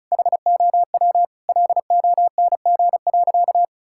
Morse code, heard as a keyed tone, about 34 wpm, 700 Hz. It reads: HOW LONG.
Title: Sentences from Top 1000 Words - Review Part 6 - 35wpm